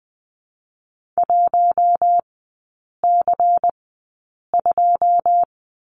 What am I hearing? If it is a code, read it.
Morse code, heard as a keyed tone, about 20 wpm, spaced as for 8 wpm, 690 Hz. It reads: 1C2
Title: 205 Lesson 33e - Letter j - Three Characters 20wpm